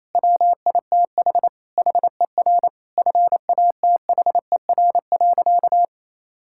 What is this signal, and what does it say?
Morse code, heard as a keyed tone, about 28 wpm, 700 Hz. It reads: WITH HER FATHER.
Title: Sentences from Top 1000 Words - Review Part 5 - 28wpm